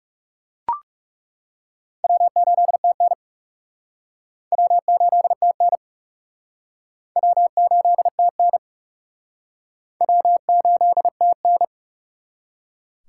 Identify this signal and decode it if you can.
Morse code, heard as a keyed tone, about 39 wpm, 700 Hz. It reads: W8TN W8TN W8TN W8TN